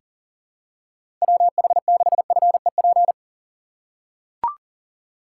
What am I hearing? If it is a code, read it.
Morse code, heard as a keyed tone, about 40 wpm, 700 Hz. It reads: WH6FEP
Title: USA Call Signs - 6 Characters - 2x Character Spacing - 40wpm